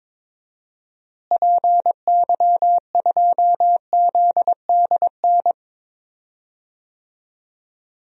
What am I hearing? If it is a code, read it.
Morse code, heard as a keyed tone, about 22 wpm, 700 Hz. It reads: PY2ZDN
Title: DXCC-100 Call Signs - 6 Characters - Repeated 3x - 22wpm